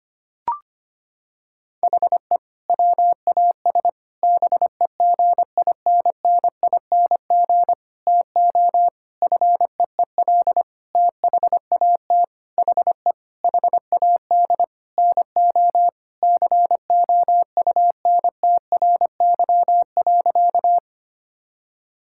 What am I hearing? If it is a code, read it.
Morse code, heard as a keyed tone, about 25 wpm, 700 Hz. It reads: HE WAS BEGINNING TO FEEL THAT HE HAD NO COUNTRY.
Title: Sentences from Top 1000 Words - Review Part 5 - 25wpm